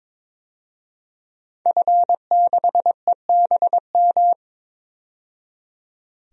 Morse code, heard as a keyed tone, about 22 wpm, 700 Hz. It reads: F6EBM
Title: DXCC-100 Call Signs - 5 Characters - 22wpm